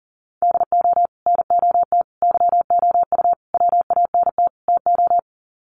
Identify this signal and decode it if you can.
Morse code, heard as a keyed tone, about 40 wpm, 700 Hz. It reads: DO NOT YOU WANT TO
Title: Sets of 5 Words Encore2 40wpm